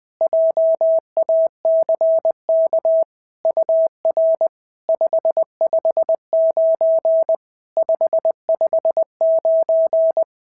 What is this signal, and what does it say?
Morse code, heard as a keyed tone, about 20 wpm, 650 Hz. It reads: JACK UR 559 559
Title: SOTA QSO Practice 20wpm